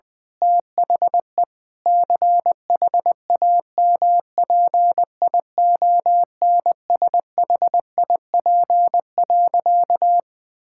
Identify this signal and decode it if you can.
Morse code, heard as a keyed tone, about 20 wpm, 700 Hz. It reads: THE CHAMPIONSHIP.
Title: Sentences with 8-Word Phrases - 20wpm